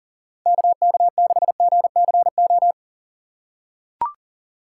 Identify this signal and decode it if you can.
Morse code, heard as a keyed tone, about 39 wpm, 700 Hz. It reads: KK6GCO
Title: USA Call Signs - 6 Characters - 6x Character Spacing - 40wpm